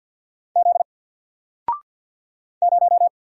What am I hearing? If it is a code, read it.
Morse code, heard as a keyed tone, about 50 wpm, 700 Hz. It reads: Z 0
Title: Single Letter-Number Alt 50wpm